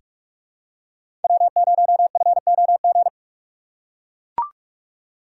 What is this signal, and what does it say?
Morse code, heard as a keyed tone, about 44 wpm, 700 Hz. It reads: W0FOG
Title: USA Call Signs - 5 Characters - 6x Character Spacing - 45wpm